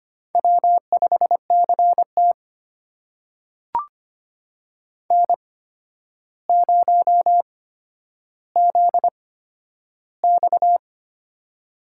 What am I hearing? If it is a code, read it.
Morse code, heard as a keyed tone, about 25 wpm, spaced as for 11 wpm, 700 Hz. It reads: W5CT N0ZX
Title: USA Call Signs - 4 Characters - 8x Character Spacing - 25wpm